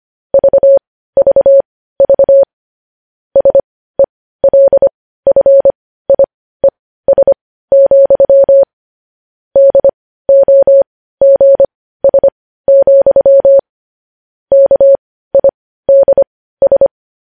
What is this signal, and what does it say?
Morse code, heard as a keyed tone, about 25 wpm, spaced as for 15 wpm, 550 Hz. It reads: VVV SELFIES, DOGS, KIDS
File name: podcast / News headlines-1x-25x15